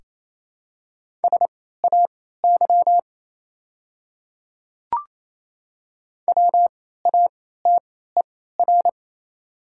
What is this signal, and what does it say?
Morse code, heard as a keyed tone, about 28 wpm, spaced as for 16 wpm, 700 Hz. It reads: SAY WATER
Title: 3x Character Spacing - Top 200 Words - 28wpm